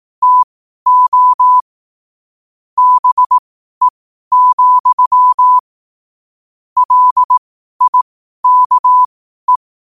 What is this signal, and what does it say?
Morse code, heard as a keyed tone, about 18 wpm, spaced as for 12 wpm, 1000 Hz. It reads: TO BE, LIKE
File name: qotdfiles / 12wpm